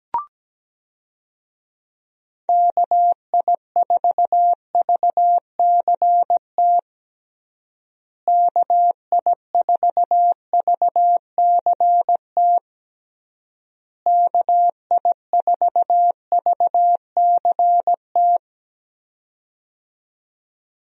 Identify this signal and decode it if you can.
Morse code, heard as a keyed tone, about 17 wpm, 700 Hz. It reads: KI4VCT KI4VCT KI4VCT
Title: USA Call Signs - 6 Characters - Repeated 3x - 17wpm